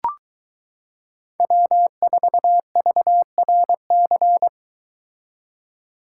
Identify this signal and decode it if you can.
Morse code, heard as a keyed tone, about 23 wpm, 700 Hz. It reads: W4VRC